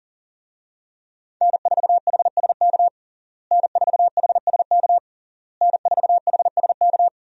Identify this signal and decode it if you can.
Morse code, heard as a keyed tone, about 40 wpm, 700 Hz. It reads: N4HSK N4HSK N4HSK
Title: USA Call Signs - 5 Characters - Repeated 3x - 40wpm